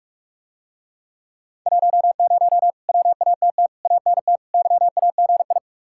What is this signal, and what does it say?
Morse code, heard as a keyed tone, about 50 wpm, 700 Hz. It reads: 10 WATT ANT YAGI